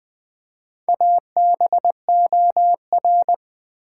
Morse code, heard as a keyed tone, about 20 wpm, 700 Hz. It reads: ABOR
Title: Build A Word - 2 Letters At A Time - 8-Letter Words - 20wpm